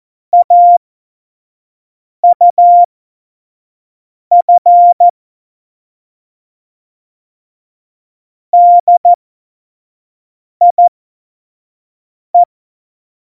Morse code, heard as a keyed tone, about 14 wpm, spaced as for 5 wpm, 700 Hz. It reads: AUF DIE
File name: Morsekurs / Witze / Witz 18 Juli 17, Tempo 25 F70